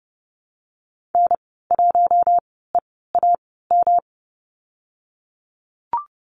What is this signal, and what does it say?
Morse code, heard as a keyed tone, about 30 wpm, 700 Hz. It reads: N 1 E A M
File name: Lesson 19g - Letter w - Call Signs 30wpm